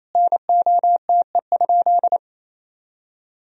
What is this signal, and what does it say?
Morse code, heard as a keyed tone, about 28 wpm, 700 Hz. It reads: NOTE?
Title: Sentences from Top 500 Words and Abbreviations 28wpm